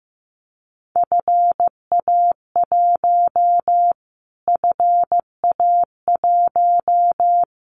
Morse code, heard as a keyed tone, about 15 wpm, 700 Hz. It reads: FA1 FA1
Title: Callsign Prefixes for Top 25 DXCC - Repeated 3x - 15wpm